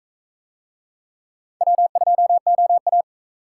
Morse code, heard as a keyed tone, about 41 wpm, 700 Hz. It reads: W2OA